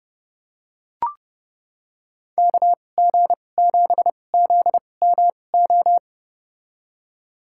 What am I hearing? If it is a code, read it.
Morse code, heard as a keyed tone, about 31 wpm, spaced as for 22 wpm, 700 Hz. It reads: KG7ZMO
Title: USA Call Signs - 2x Character Spacing - 30wpm